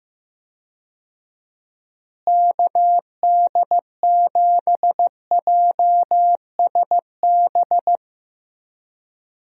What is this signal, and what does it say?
Morse code, heard as a keyed tone, about 15 wpm, 700 Hz. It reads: KD7JSB
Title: USA Call Signs - 6 Characters - Repeated 3x - 15wpm